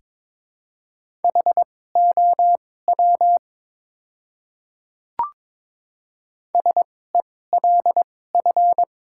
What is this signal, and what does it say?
Morse code, heard as a keyed tone, about 22 wpm, spaced as for 16 wpm, 700 Hz. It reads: HOW SELF
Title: 2x Character Spacing - Top 200 Words - 22wpm